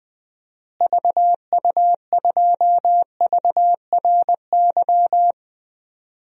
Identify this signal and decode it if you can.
Morse code, heard as a keyed tone, about 21 wpm, 700 Hz. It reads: VU2VRY
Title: DXCC-100 Call Signs - 6 Characters - Repeated 2x - 20wpm